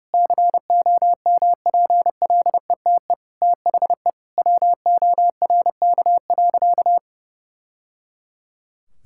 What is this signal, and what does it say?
Morse code, heard as a keyed tone, about 30 wpm, 700 Hz. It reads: COMPLETE THE WORK.
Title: Sentences from Top 1000 Words - Review Part 6 - 30wpm